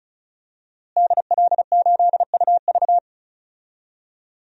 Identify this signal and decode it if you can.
Morse code, heard as a keyed tone, about 36 wpm, 700 Hz. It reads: DL8UV
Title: DXCC-100 Call Signs - 5 Characters - Repeated 2x - 35wpm